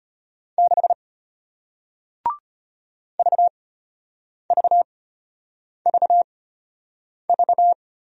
Morse code, heard as a keyed tone, about 33 wpm, spaced as for 8 wpm, 700 Hz. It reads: 6 VVVV